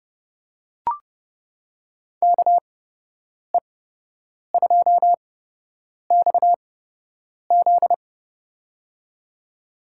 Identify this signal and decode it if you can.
Morse code, heard as a keyed tone, about 32 wpm, spaced as for 8 wpm, 700 Hz. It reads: KE2XZ
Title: USA Call Signs - 5 Characters - 8x Character Spacing - 30wpm